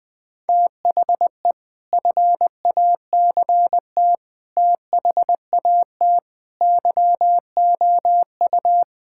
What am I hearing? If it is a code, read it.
Morse code, heard as a keyed tone, about 21 wpm, 700 Hz. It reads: THE FACT THAT YOU
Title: Sentences with 3-Word Phrases - 20wpm